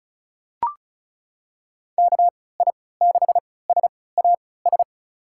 Morse code, heard as a keyed tone, about 36 wpm, spaced as for 20 wpm, 700 Hz. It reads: KI6SAS
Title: USA Call Signs - 6 Characters - 3x Character Spacing - 35wpm